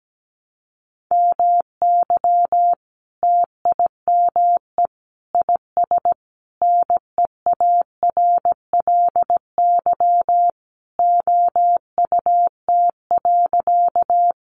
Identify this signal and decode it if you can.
Morse code, heard as a keyed tone, about 17 wpm, 700 Hz. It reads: MY TIME IS NEARLY OUT.
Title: Sentences from Top 1300 Words 17wpm